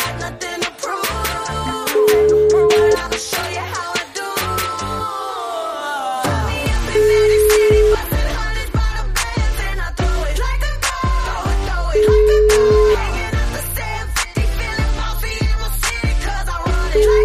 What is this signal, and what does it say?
Ringback tone (Europe, 425 Hz), with noise.